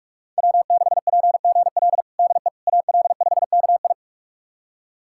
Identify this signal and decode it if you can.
Morse code, heard as a keyed tone, about 46 wpm, 700 Hz. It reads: W6PGL DE AL5KI
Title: USA Callsign DE Callsign - 5 Characters - 45wpm